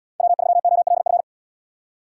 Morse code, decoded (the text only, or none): W3OLF